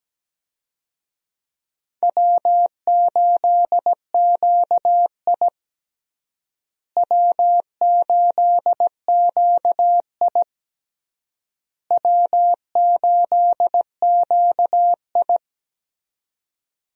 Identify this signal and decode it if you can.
Morse code, heard as a keyed tone, about 17 wpm, 700 Hz. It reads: W8QI W8QI W8QI
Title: USA Call Signs - 4 Characters - Repeated 3x - 17wpm